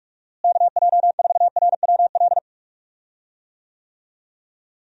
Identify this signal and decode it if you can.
Morse code, heard as a keyed tone, about 46 wpm, 700 Hz. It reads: KJ4RWL
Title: USA Call Signs - 6 Characters - Repeated 3x - 45wpm